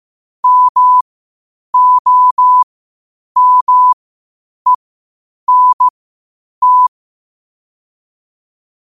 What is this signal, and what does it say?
Morse code, heard as a keyed tone, about 15 wpm, spaced as for 8 wpm, 1000 Hz. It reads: MOMENT
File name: qotdfiles / 8wpm